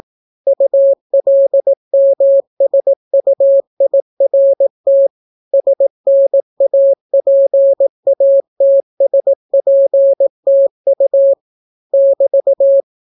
Morse code, heard as a keyed tone, about 18 wpm, 550 Hz. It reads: ULMSUIRT SNAPATSPTU <BT>